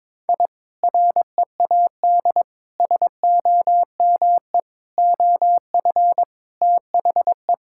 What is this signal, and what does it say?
Morse code, heard as a keyed tone, about 21 wpm, 700 Hz. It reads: I READ SOME OF THE
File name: practice / 22wpm / Sentences with 2-Word Phrases - 22wpm